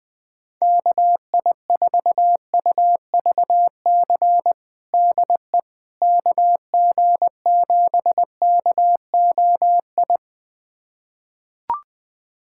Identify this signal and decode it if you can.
Morse code, heard as a keyed tone, about 20 wpm, 700 Hz. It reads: KI4UVC DE KG7KOI
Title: USA Callsign DE Callsign - 6 Characters - 20wpm